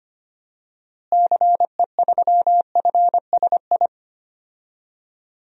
Morse code, heard as a keyed tone, about 25 wpm, 700 Hz. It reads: CE3FSI
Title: DXCC-100 Call Signs - 6 Characters - Repeated 2x - 25wpm